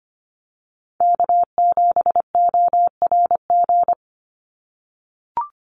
Morse code, heard as a keyed tone, about 25 wpm, 700 Hz. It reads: K7ORG
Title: NA SOTA Chaser Callsigns 25wpm